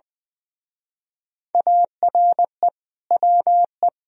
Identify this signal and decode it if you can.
Morse code, heard as a keyed tone, about 20 wpm, 700 Hz. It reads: ARE WE
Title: Speed-Racing - Sets of 4 Words 20wpm